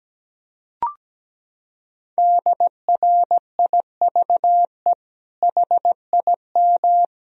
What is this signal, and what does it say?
Morse code, heard as a keyed tone, about 17 wpm, 700 Hz. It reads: DRIVE HIM
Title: Sentences from Top 1000 Words - Review Part 2 - 17wpm